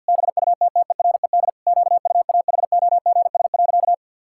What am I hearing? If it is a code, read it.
Morse code, heard as a keyed tone, about 53 wpm, 700 Hz. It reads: BUTTERED QUAHOGS.
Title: Pangram Challenge v2 - 50wpm